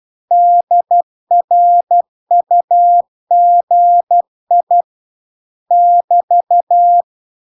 Morse code, heard as a keyed tone, about 12 wpm, 700 Hz. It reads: DRUGI <BT>